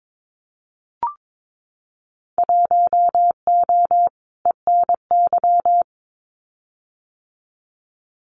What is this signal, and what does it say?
Morse code, heard as a keyed tone, about 22 wpm, 700 Hz. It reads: 1O ENY
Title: Winter Field Day Exchanges 22wpm